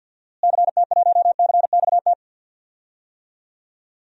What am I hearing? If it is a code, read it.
Morse code, heard as a keyed tone, about 49 wpm, 700 Hz. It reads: CT1CXT
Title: DXCC-100 Call Signs - 6 Characters - Repeated 3x - 50wpm